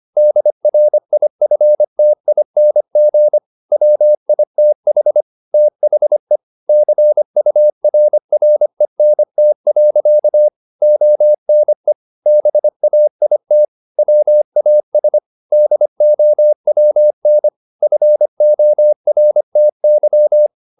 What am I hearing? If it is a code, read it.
Morse code, heard as a keyed tone, about 25 wpm, 600 Hz. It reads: DRIFTING WITH THE CURRENT. ONE BAIT WAS DOWN FORTY
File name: the old man and the sea part 1 25wpm